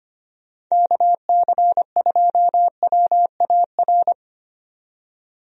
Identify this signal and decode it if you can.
Morse code, heard as a keyed tone, about 25 wpm, 700 Hz. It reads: KC2WAR
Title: USA Call Signs - 6 Characters - 25wpm